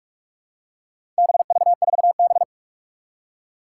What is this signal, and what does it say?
Morse code, heard as a keyed tone, about 44 wpm, 700 Hz. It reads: BV4B
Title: Four Challenging Letter-Number Combinations 45wpm